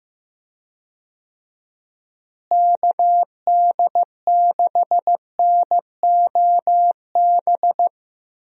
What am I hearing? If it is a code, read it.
Morse code, heard as a keyed tone, about 15 wpm, 700 Hz. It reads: KD6NOB
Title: USA Call Signs - 6 Characters - Repeated 3x - 15wpm